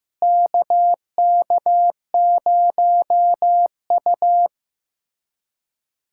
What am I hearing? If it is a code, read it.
Morse code, heard as a keyed tone, about 15 wpm, 700 Hz. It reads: KK0U
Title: USA Callsign DE Callsign - 4 Characters - 15wpm